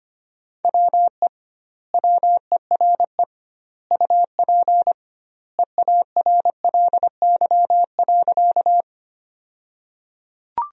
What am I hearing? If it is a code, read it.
Morse code, heard as a keyed tone, about 25 wpm, 700 Hz. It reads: WE WERE UP EARLY.
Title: 2x Word-Spacing - Sentences from Top 400 Words - 25wpm